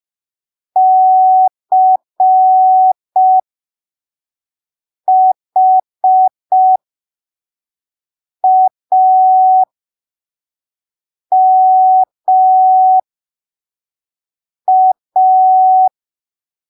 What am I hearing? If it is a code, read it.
Morse code, heard as a keyed tone, about 5 wpm, spaced as for 3 wpm, 750 Hz. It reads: CHAMA
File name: arquivos / texto / texto 29 05ppm